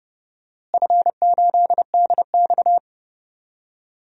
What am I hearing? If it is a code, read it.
Morse code, heard as a keyed tone, about 32 wpm, 700 Hz. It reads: F8DX
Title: DXCC-100 Call Signs - 4 Characters - Repeated 3x - 30wpm